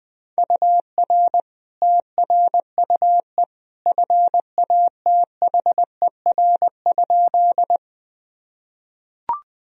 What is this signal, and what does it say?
Morse code, heard as a keyed tone, about 21 wpm, 700 Hz. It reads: UR TRUE FATHER?
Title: Sentences from Top 400 Words and Abbreviations 20wpm